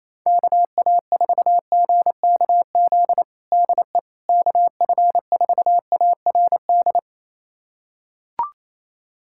Morse code, heard as a keyed tone, about 28 wpm, 700 Hz. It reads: KA4GKZ DE KF4ARD
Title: USA Callsign DE Callsign - 6 Characters - 28wpm